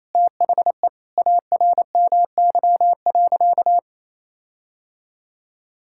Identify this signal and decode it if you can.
Morse code, heard as a keyed tone, about 27 wpm, 700 Hz. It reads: THE ARMY.
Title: Sentences from Top 1600 Words 28wpm